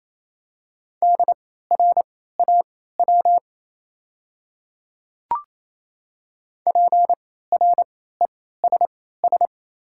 Morse code, heard as a keyed tone, about 28 wpm, spaced as for 16 wpm, 700 Hz. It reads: DRAW PRESS
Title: 3x Character Spacing - Top 300 Words - 28wpm